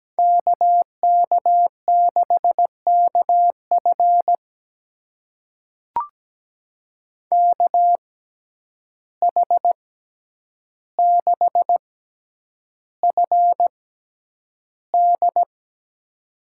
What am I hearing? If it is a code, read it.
Morse code, heard as a keyed tone, about 17 wpm, spaced as for 9 wpm, 700 Hz. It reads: KK6KF KH6FD
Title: USA Call Signs - 6x Character Spacing - 17wpm